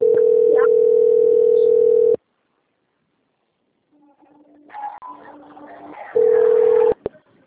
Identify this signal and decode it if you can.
Ringback tone (North America), 440 + 480 Hz.